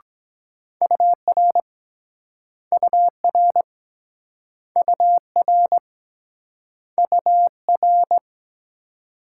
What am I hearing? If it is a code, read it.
Morse code, heard as a keyed tone, about 24 wpm, 700 Hz. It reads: UR UR UR UR